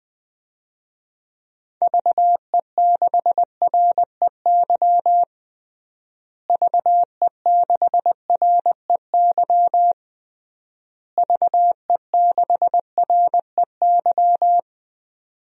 Morse code, heard as a keyed tone, about 19 wpm, 700 Hz. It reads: VE6REY VE6REY VE6REY
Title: DXCC-100 Call Signs - 6 Characters - Repeated 3x - 20wpm